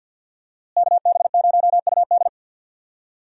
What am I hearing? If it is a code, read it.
Morse code, heard as a keyed tone, about 46 wpm, 700 Hz. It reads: KB0UD
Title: USA Call Signs - 5 Characters - Repeated 3x - 50wpm